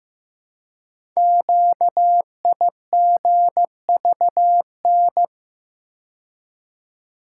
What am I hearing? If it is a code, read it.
Morse code, heard as a keyed tone, about 15 wpm, 700 Hz. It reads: QIGVN
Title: Code Groups - 5 Letters - 15wpm